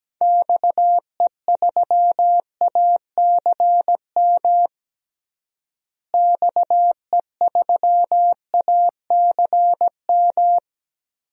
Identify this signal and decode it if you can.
Morse code, heard as a keyed tone, about 17 wpm, 700 Hz. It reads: XE3ACM XE3ACM
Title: DXCC-100 Call Signs - 6 Characters - Repeated 3x - 17wpm